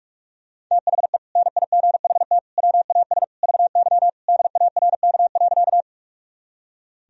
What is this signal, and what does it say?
Morse code, heard as a keyed tone, about 46 wpm, 700 Hz. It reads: THE NIGHT WAS VY DARK.
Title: Sentences from Top 500 Words and Abbreviations 45wpm